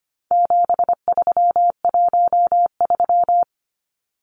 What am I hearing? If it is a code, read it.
Morse code, heard as a keyed tone, about 25 wpm, 700 Hz. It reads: 7313
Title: Numbers - 4-Digit 25wpm